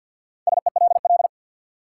Morse code, heard as a keyed tone, about 47 wpm, 700 Hz. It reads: SELL
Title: Sentences from Top 1100 Words - New Word Drill Alt - 50wpm